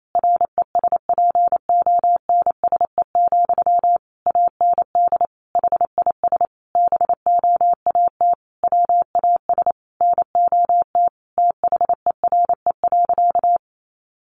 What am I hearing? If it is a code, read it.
Morse code, heard as a keyed tone, about 28 wpm, 700 Hz. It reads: RESPONSE, AND HIS BOAT WAS NOT THERE.
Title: Sentences from Top 1400 Words 28wpm